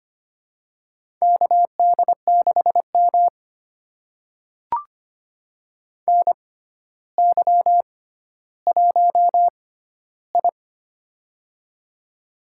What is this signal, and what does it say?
Morse code, heard as a keyed tone, about 25 wpm, spaced as for 13 wpm, 700 Hz. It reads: KD6M NY1I
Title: USA Call Signs - 4 Characters - 6x Character Spacing - 25wpm